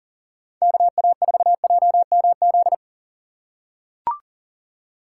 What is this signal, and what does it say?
Morse code, heard as a keyed tone, about 38 wpm, 700 Hz. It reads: KA4JMZ
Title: USA Call Signs - 6 Characters - 2x Character Spacing - 40wpm